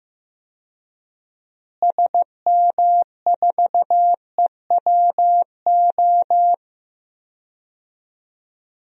Morse code, heard as a keyed tone, about 15 wpm, 700 Hz. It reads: SM4EWO
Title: DXCC-100 Call Signs - 6 Characters - Repeated 3x - 15wpm